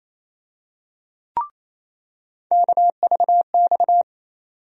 Morse code, heard as a keyed tone, about 28 wpm, 700 Hz. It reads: KVX
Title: USA Call Sign Postfixes - 3 Letters 28wpm